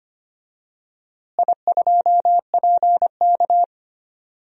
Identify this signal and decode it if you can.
Morse code, heard as a keyed tone, about 25 wpm, 700 Hz. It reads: I2PK
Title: DXCC-100 Call Signs - 4 Characters - Repeated 3x - 25wpm